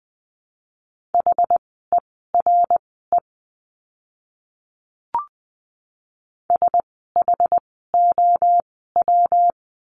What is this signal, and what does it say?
Morse code, heard as a keyed tone, about 20 wpm, spaced as for 14 wpm, 700 Hz. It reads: HERE SHOW
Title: 244 Lesson 39f - Letter z - Words 20wpm